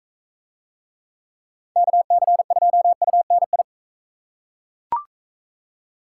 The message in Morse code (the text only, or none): KC2UNI